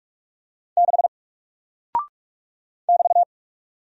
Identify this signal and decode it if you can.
Morse code, heard as a keyed tone, about 43 wpm, 700 Hz. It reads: 6 <BT>
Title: Single Character (Letter:Number:Punctuation:Prosign) Alt 45wpm